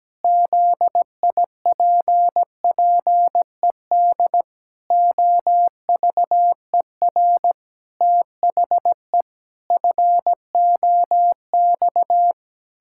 Morse code, heard as a keyed tone, about 17 wpm, 700 Hz. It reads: ZIPPED OVER THE FOX
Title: Pangram Challenge v2 - 17wpm